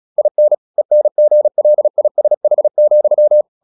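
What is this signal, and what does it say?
Morse code, heard as a keyed tone, about 36 wpm, 600 Hz. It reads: IN ENGLISH,
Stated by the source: Morse practice at 36 wpm, text of Ken Follett's novel Never, from CW med Gnister.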